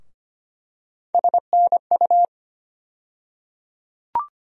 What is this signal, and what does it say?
Morse code, heard as a keyed tone, about 25 wpm, 700 Hz. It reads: SNU
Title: USA Call Sign Postfixes - 3 Letters 25wpm